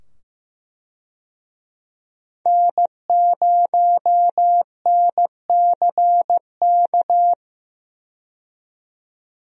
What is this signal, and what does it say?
Morse code, heard as a keyed tone, about 15 wpm, 700 Hz. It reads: N0NCK
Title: USA Call Signs - 5 Characters - 15wpm